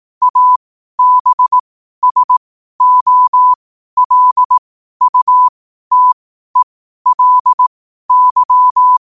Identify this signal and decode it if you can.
Morse code, heard as a keyed tone, about 18 wpm, spaced as for 12 wpm, 1000 Hz. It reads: ABSOLUTELY
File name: qotdfiles / 12wpm